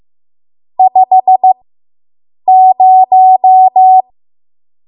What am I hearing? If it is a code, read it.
Morse code, heard as a keyed tone, about 15 wpm, spaced as for 7 wpm, 750 Hz. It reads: 50